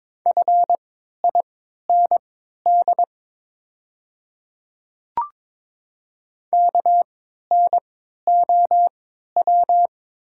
Morse code, heard as a keyed tone, about 22 wpm, spaced as for 13 wpm, 700 Hz. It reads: FIND KNOW
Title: 3x Character Spacing - Top 200 Words - 22wpm